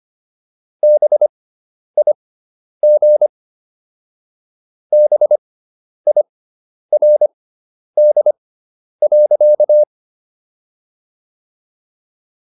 Morse code, heard as a keyed tone, about 25 wpm, spaced as for 10 wpm, 600 Hz. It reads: BIG BIRD.